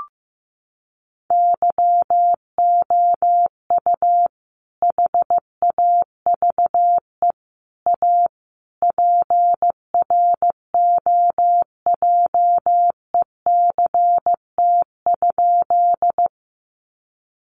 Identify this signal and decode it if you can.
Morse code, heard as a keyed tone, about 14 wpm, 700 Hz. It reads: YOU HAVE A PROJECT?
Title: Sentences from Top 1400 Words 15wpm